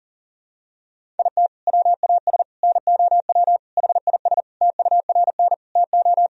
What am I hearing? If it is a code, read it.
Morse code, heard as a keyed tone, about 40 wpm, 700 Hz. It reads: IT WAS NOW HIS TURN TO
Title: Sentences from Top 200 Words and Abbreviations 40wpm